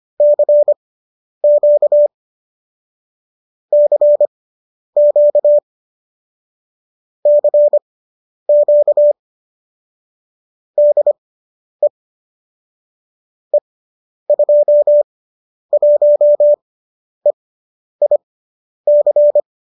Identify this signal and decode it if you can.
Morse code, heard as a keyed tone, about 25 wpm, spaced as for 10 wpm, 570 Hz. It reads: CQ CQ CQ DE E21EIC
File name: qso005 10